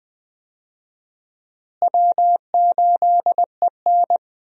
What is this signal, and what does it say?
Morse code, heard as a keyed tone, about 21 wpm, 700 Hz. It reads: W8EN
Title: USA Call Signs - 4 Characters - Repeated 3x - 20wpm